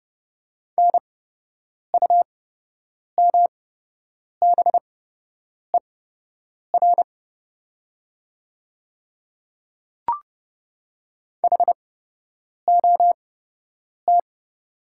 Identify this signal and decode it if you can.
Morse code, heard as a keyed tone, about 31 wpm, spaced as for 8 wpm, 700 Hz. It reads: NUMBER HOT
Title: 8x Character Spacing - Top 100 Words - 30wpm